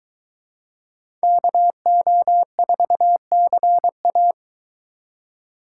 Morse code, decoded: KO4CA

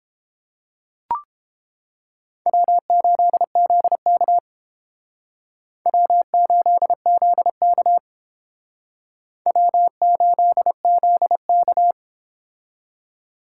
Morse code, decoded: W8ZK W8ZK W8ZK

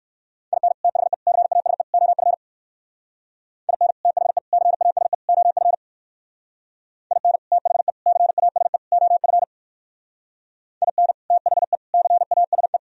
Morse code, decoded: IN THE CASE OF IN THE CASE OF IN THE CASE OF IN THE CASE

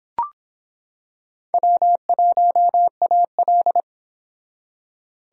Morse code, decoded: W1AL

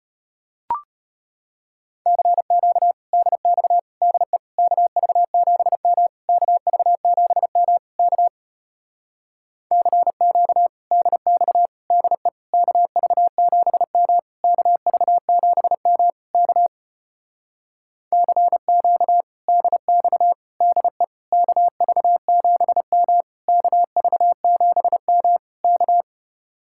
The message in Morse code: CQ DX DE KV7M KV7M K CQ DX DE KV7M KV7M K CQ DX DE KV7M KV7M K

